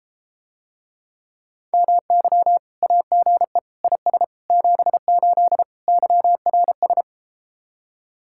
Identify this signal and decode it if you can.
Morse code, heard as a keyed tone, about 34 wpm, 700 Hz. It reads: MY AGE IS 78 YRS